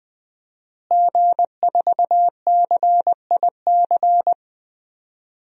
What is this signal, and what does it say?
Morse code, heard as a keyed tone, about 21 wpm, 700 Hz. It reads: G4CIC
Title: DXCC-100 Call Signs - 5 Characters - Repeated 3x - 20wpm